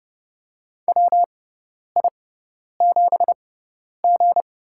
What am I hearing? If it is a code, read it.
Morse code, heard as a keyed tone, about 30 wpm, spaced as for 10 wpm, 700 Hz. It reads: WI7G
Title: USA Call Signs - 6x Character Spacing - 30wpm